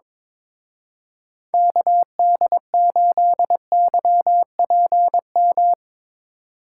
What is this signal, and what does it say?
Morse code, heard as a keyed tone, about 24 wpm, 700 Hz. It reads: KD8YPM
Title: Speed Racing - USA 2x3 Call Signs 22wpm